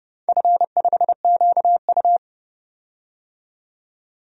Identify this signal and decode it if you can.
Morse code, heard as a keyed tone, about 31 wpm, 700 Hz. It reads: F5QU
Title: DXCC-100 Call Signs - 4 Characters - 30wpm